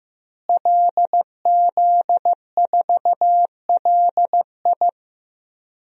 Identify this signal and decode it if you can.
Morse code, heard as a keyed tone, about 15 wpm, 700 Hz. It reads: LZ4LI
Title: DXCC-100 Call Signs - 5 Characters - Repeated 3x - 15wpm